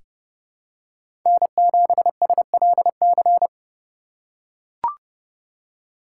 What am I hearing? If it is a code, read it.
Morse code, heard as a keyed tone, about 30 wpm, 700 Hz. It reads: N7SLC
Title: USA Call Signs - 5 Characters - 2x Character Spacing - 30wpm